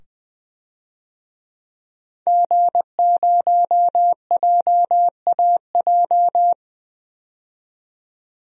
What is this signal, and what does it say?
Morse code, heard as a keyed tone, about 20 wpm, 700 Hz. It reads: G0JAJ
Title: DXCC-100 Call Signs - 5 Characters - 20wpm